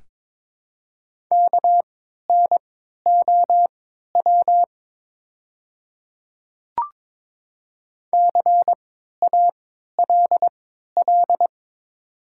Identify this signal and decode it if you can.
Morse code, heard as a keyed tone, about 22 wpm, spaced as for 12 wpm, 700 Hz. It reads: KNOW CALL
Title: 3x Character Spacing - Top 100 Words - 22wpm